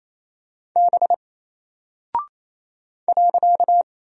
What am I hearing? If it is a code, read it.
Morse code, heard as a keyed tone, about 27 wpm, 700 Hz. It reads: B .
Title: Single Character (Letter:Number:Punctuation:Prosign) 28wpm